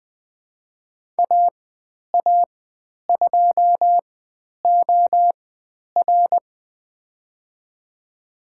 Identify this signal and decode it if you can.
Morse code, heard as a keyed tone, about 20 wpm, spaced as for 10 wpm, 700 Hz. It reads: AA2OR
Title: 095 Lesson 16g - Letter c - Call Signs 20wpm